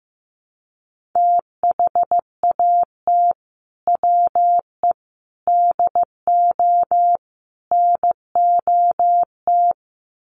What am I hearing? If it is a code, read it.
Morse code, heard as a keyed tone, about 15 wpm, 700 Hz. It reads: THAT WE DO NOT
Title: Sets of 4 Words 15wpm